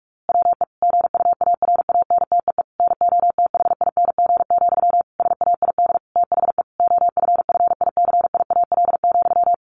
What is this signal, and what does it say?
Morse code, heard as a keyed tone, about 44 wpm, 700 Hz. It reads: WE GUARANTEE NOTHING, SAID THE OFFICIAL,
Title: A Sound Of Thunder 45wpm